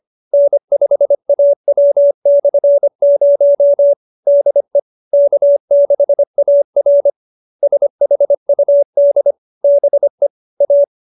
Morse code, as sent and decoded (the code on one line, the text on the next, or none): -. ..... .- .-- -..-. ----- / -.. / -.- -.... .- .-. / ... .... ..- -.. / -... / .-
N5AW/0 DE K6AR SHUD BE A